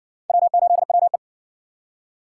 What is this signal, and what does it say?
Morse code, heard as a keyed tone, about 60 wpm, 700 Hz. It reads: W8WE